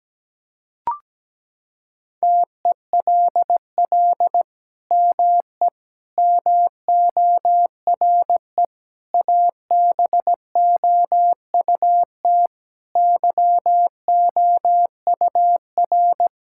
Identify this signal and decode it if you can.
Morse code, heard as a keyed tone, about 17 wpm, 700 Hz. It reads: TELL ME MORE ABOUT YOUR
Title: Sentences from Top 1000 Words - Review Part 6 - 17wpm